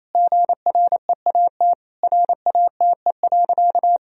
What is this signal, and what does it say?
Morse code, heard as a keyed tone, about 28 wpm, 700 Hz. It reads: GREAT RATE.
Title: Sentences from Top 1000 Words - Review Part 4 - 28wpm